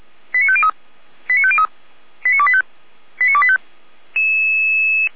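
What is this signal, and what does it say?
Five-tone selective call (ZVEI), heard as quick runs of five tones, tones of about 70 ms. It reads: 89503 89503 89285 89285
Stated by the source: ZVEI five-tone call received on SDR, fire brigade Orlamünde.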